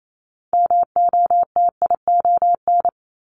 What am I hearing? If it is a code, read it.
Morse code, heard as a keyed tone, about 28 wpm, 700 Hz. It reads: MOTION
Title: Multi-letter Phonemes 28wpm